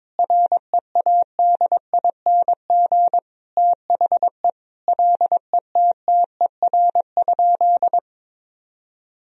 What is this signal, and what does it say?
Morse code, heard as a keyed tone, about 21 wpm, 700 Hz. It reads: READING THE LETTER?
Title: Sentences from Top 1000 Words - Review Part 3 - 22wpm